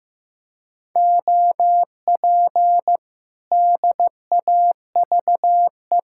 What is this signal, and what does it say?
Morse code, heard as a keyed tone, about 15 wpm, 700 Hz. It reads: OP DAVE
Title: Speed-Racing Common QSO Phrases Encore - 15wpm